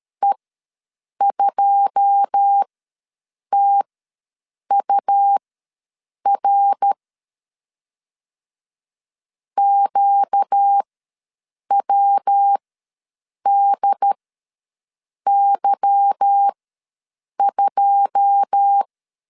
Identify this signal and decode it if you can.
Morse code, heard as a keyed tone, about 13 wpm, spaced as for 7 wpm, 800 Hz. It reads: E2TUR QWDY2
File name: LEZIONI / 9 2 ws 12wpm 3406